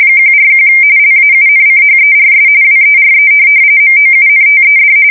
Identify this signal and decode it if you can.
RTTY, about 45.45 baud. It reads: 141 LAUNCHERS WERE MOUNTED
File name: fsk170 / USS-New-Jersey